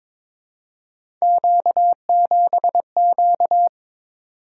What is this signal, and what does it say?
Morse code, heard as a keyed tone, about 22 wpm, 700 Hz. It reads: Q7Q
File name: practice / 22wpm / Three Letter-Number Combinations - 22wpm